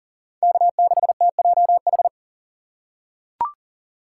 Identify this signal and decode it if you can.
Morse code, heard as a keyed tone, about 40 wpm, 700 Hz. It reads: K6TJH
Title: USA Call Signs - 5 Characters - 4x Character Spacing - 40wpm